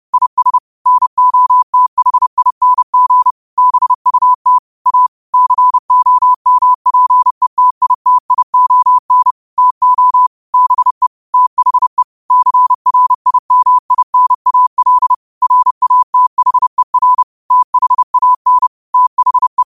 Morse code, heard as a keyed tone, about 30 wpm, 1000 Hz. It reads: IS NOTHING BUT A COMPETITION TO BE THE CRIMINAL RATHER THAN THE
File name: 30wpm